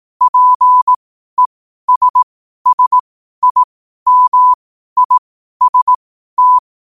Morse code, heard as a keyed tone, about 18 wpm, spaced as for 12 wpm, 1000 Hz. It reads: PESSIMIST
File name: qotdfiles / 12wpm